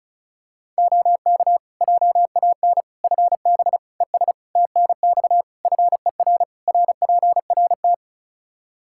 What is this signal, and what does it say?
Morse code, heard as a keyed tone, about 34 wpm, 700 Hz. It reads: OK JAN FB ES TNX FER RPRT
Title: K1USN SST QSO Practice 35wpm